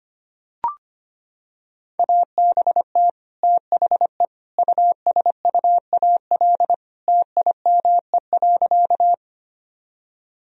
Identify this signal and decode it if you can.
Morse code, heard as a keyed tone, about 25 wpm, 700 Hz. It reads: ABT THE USUAL TIME.
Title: Sentences from Top 400 Words and Abbreviations 25wpm